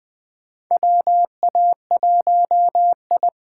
Morse code, heard as a keyed tone, about 20 wpm, 700 Hz. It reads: WA1I
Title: USA Call Signs - 6 Characters - 8x Character Spacing - 20wpm